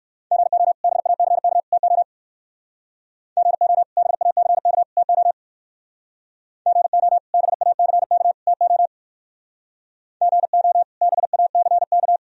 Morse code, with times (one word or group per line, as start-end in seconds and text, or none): 0.3-0.7 GO
0.8-1.6 BACK
1.7-2.0 TO
3.4-3.8 GO
4.0-4.8 BACK
5.0-5.3 TO
6.7-7.2 GO
7.3-8.3 BACK
8.5-8.9 TO
10.2-10.8 GO
11.0-12.2 BACK